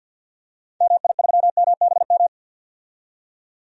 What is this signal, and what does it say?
Morse code, heard as a keyed tone, about 52 wpm, 700 Hz. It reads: MI3MBM
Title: DXCC-100 Call Signs - 6 Characters - 50wpm